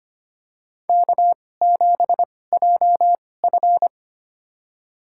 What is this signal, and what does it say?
Morse code, heard as a keyed tone, about 25 wpm, spaced as for 18 wpm, 700 Hz. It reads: K7JF
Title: USA Call Signs - 4 Characters - 2x Character Spacing - 25wpm